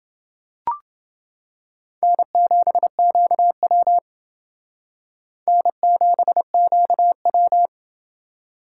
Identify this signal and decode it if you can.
Morse code, heard as a keyed tone, about 28 wpm, 700 Hz. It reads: N7QW N7QW